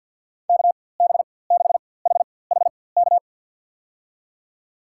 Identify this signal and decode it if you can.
Morse code, heard as a keyed tone, about 48 wpm, spaced as for 23 wpm, 700 Hz. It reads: KB6HHK
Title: USA Call Signs - 6 Characters - 4x Character Spacing - 50wpm